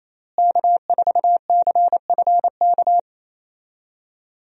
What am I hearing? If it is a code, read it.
Morse code, heard as a keyed tone, about 28 wpm, 700 Hz. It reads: K4CFK
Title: USA Call Signs - 5 Characters - Repeated 3x - 28wpm